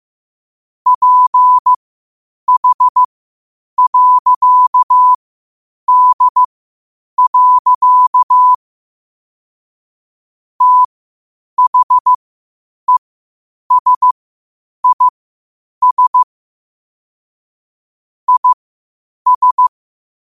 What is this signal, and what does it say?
Morse code, heard as a keyed tone, about 15 wpm, spaced as for 8 wpm, 1000 Hz. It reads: PH.D. THESIS IS
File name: qotdfiles / 8wpm